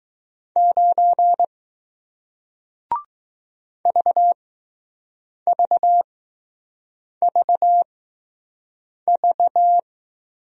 Morse code, heard as a keyed tone, about 20 wpm, spaced as for 6 wpm, 700 Hz. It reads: 9 VVVV